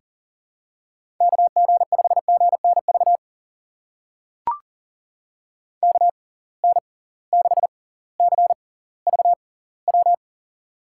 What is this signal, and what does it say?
Morse code, heard as a keyed tone, about 41 wpm, spaced as for 21 wpm, 700 Hz. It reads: KG5GNV KN6CVW
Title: USA Call Signs - 6 Characters - 6x Character Spacing - 40wpm